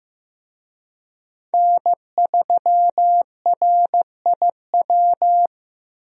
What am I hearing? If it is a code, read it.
Morse code, heard as a keyed tone, about 15 wpm, 700 Hz. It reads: N3RIW
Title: USA Call Signs - 5 Characters - Repeated 3x - 15wpm